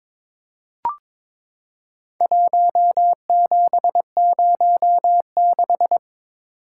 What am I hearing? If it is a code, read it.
Morse code, heard as a keyed tone, about 22 wpm, 700 Hz. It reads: 1706
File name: practice / 22wpm / Four Letter-Number Combinations - 22wpm